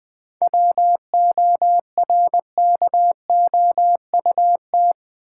Morse code, heard as a keyed tone, about 20 wpm, 690 Hz. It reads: WORKOUT
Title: 7 Letter Words - First 3 Letters Given - 20wpm